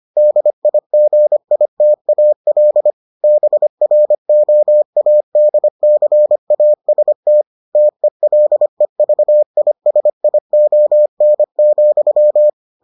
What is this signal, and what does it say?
Morse code, heard as a keyed tone, about 25 wpm, 600 Hz. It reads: DIGITAL BROADCAST TELEVISION,